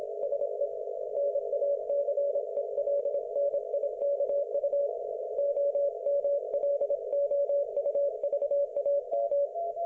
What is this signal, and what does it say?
Morse code, heard as a keyed tone, about 25 wpm, 570 Hz. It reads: SE NACELNIKU OMLOUVAM
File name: Archiv / CUC216 / ok1fhi cul216